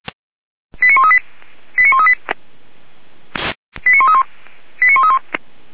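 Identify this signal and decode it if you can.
Five-tone selective call (ZVEI), heard as quick runs of five tones, tones of about 70 ms. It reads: 89138 89138 89131 89131
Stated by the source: Five-tone ZVEI selective call 89132 received on an SDR, from the feuerwehr-orlamuende.org audio server.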